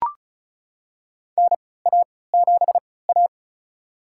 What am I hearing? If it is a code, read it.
Morse code, heard as a keyed tone, about 36 wpm, spaced as for 20 wpm, 700 Hz. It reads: NA7A
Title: USA Call Signs - 4 Characters - 3x Character Spacing - 35wpm